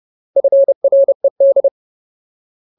Morse code, heard as a keyed tone, about 30 wpm, 550 Hz. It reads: FRED
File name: Ultimate Word List / 30WPM Ultimate